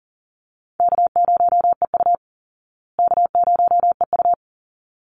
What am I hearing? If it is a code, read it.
Morse code, heard as a keyed tone, about 40 wpm, 700 Hz. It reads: K0EU K0EU
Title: Easy Call Signs Repeated 2x - 40wpm